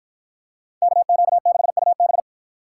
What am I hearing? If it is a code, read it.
Morse code, heard as a keyed tone, about 49 wpm, 700 Hz. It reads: KQ6UB